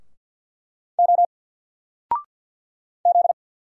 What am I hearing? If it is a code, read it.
Morse code, heard as a keyed tone, about 50 wpm, 700 Hz. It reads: O Z
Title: Single Letter-Number 50wpm